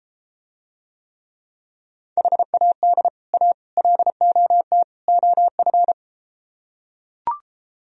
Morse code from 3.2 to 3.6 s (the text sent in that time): A